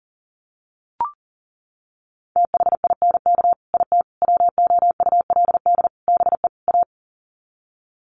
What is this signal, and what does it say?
Morse code, heard as a keyed tone, about 40 wpm, 700 Hz. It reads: THINK IT WOULD BE A
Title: Sets of 5 Words Encore2 40wpm